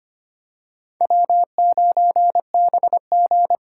Morse code, heard as a keyed tone, about 25 wpm, 700 Hz. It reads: W9BG
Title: USA Call Signs - 4 Characters - 4x Character Spacing - 25wpm